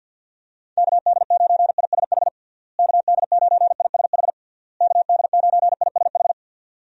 Morse code, heard as a keyed tone, about 48 wpm, 700 Hz. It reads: KD9ISH KD9ISH KD9ISH
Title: USA Call Signs - 6 Characters - Repeated 3x - 50wpm